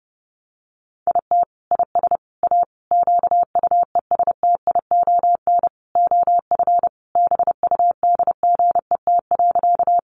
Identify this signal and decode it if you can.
Morse code, heard as a keyed tone, about 30 wpm, 700 Hz. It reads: IT IS A QUESTION OF BUDGET.
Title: Sentences from Top 1200 Words 30wpm